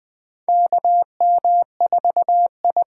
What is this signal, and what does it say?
Morse code, heard as a keyed tone, about 20 wpm, 700 Hz. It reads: KM4I
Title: USA Call Signs - 6 Characters - 8x Character Spacing - 20wpm